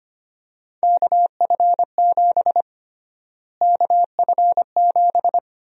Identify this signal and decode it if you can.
Morse code, heard as a keyed tone, about 25 wpm, 700 Hz. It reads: KF7 KF7
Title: USA Call Sign Prefixes - Repeated 2x - 25wpm